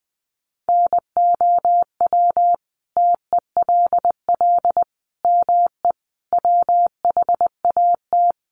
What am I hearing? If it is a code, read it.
Morse code, heard as a keyed tone, about 20 wpm, 700 Hz. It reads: NOW TELL ME WHAT
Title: Sentences from Top 1300 Words 20wpm